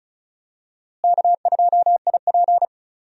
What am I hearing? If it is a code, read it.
Morse code, heard as a keyed tone, about 36 wpm, 700 Hz. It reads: K2IP
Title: USA Call Signs - 4 Characters - Repeated 3x - 35wpm